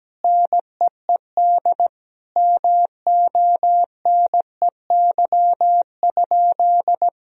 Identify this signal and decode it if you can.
Morse code, heard as a keyed tone, about 17 wpm, 700 Hz. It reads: NEED MONEY?
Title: Sentences from Top 1000 Words - Review Part 1 - 17wpm